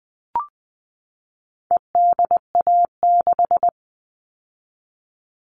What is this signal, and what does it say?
Morse code, heard as a keyed tone, about 20 wpm, 700 Hz. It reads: EDA6
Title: All Cut Numbers - 4-Digit 20wpm